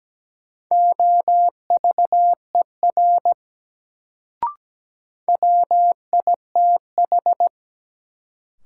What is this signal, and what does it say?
Morse code, heard as a keyed tone, about 17 wpm, 700 Hz. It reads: OVER WITH
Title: Sentences from Top 100 Words - New Word Drill - 17wpm